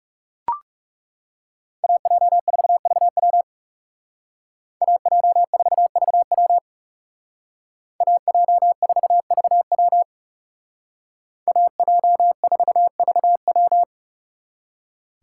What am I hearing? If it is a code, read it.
Morse code, heard as a keyed tone, about 39 wpm, 700 Hz. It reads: AJ4VW AJ4VW AJ4VW AJ4VW